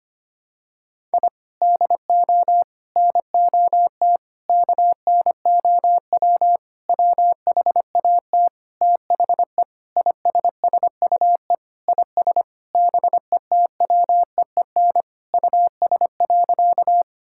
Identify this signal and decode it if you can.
Morse code, heard as a keyed tone, about 25 wpm, 700 Hz. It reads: I DO NOT KNOW WHAT THE ISSUE IS BETWEEN US.
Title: Sentences from Top 1000 Words - Review Part 6 - 25wpm